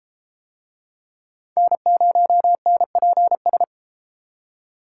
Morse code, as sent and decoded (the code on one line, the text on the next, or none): -. ----- -. .--. ...
N0NPS